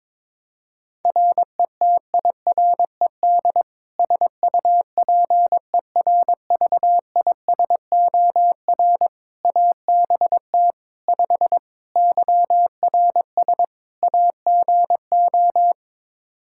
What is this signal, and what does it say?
Morse code, heard as a keyed tone, about 22 wpm, 700 Hz. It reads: RETIRED SUPERVISOR ABT 5 YRS AGO
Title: Practice QSO - Exchange 3 - More Info - 22wpm